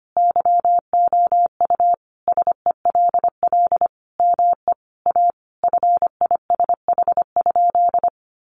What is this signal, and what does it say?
Morse code, heard as a keyed tone, about 25 wpm, 700 Hz. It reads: YOU SELL ME A FISH?
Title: Sentences from Top 1100 Words 25wpm